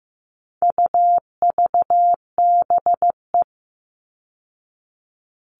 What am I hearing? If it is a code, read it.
Morse code, heard as a keyed tone, about 15 wpm, 700 Hz. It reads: UVBE
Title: All Cut Numbers - 4-Digit 15wpm